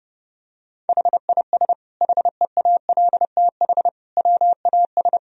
Morse code, heard as a keyed tone, about 30 wpm, 700 Hz. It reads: HIS HEALTH WAS
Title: Sentences from Top 1000 Words - Review Part 3 - 30wpm